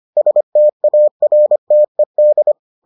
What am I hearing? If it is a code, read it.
Morse code, heard as a keyed tone, about 25 wpm, 600 Hz. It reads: STARTED